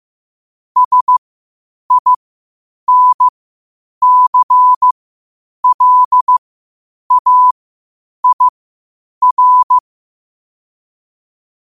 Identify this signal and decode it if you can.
Morse code, heard as a keyed tone, about 15 wpm, spaced as for 8 wpm, 1000 Hz. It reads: SINCLAIR
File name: qotdfiles / 8wpm